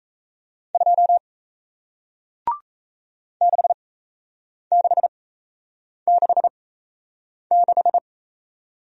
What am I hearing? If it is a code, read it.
Morse code, heard as a keyed tone, about 37 wpm, spaced as for 8 wpm, 700 Hz. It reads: 2 6666